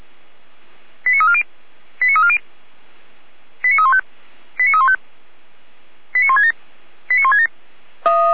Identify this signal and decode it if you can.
Five-tone selective call (ZVEI), heard as quick runs of five tones, tones of about 70 ms. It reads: 89340 89340 89315 89315 89167 89167